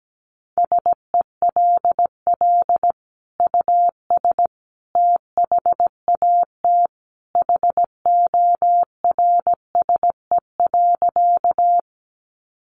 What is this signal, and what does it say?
Morse code, heard as a keyed tone, about 17 wpm, 700 Hz. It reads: SELL US THAT HORSE.
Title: Sentences from Top 1100 Words 17wpm